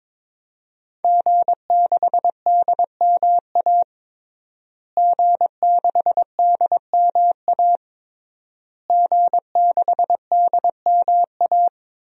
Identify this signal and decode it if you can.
Morse code, heard as a keyed tone, about 22 wpm, 700 Hz. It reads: G6DMA G6DMA G6DMA
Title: DXCC-100 Call Signs - 5 Characters - Repeated 3x - 22wpm